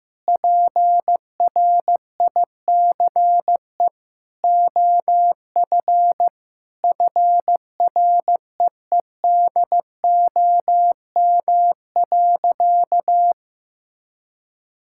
Morse code, heard as a keyed tone, about 15 wpm, 700 Hz. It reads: PRICE OF FREEDOM.
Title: Sentences from Top 1000 Words - Review Part 5 - 15wpm